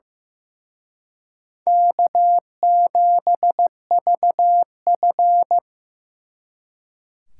Morse code, heard as a keyed tone, about 17 wpm, 700 Hz. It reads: K7VF
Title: Speed Racing - USA 1x2 Call Signs 15wpm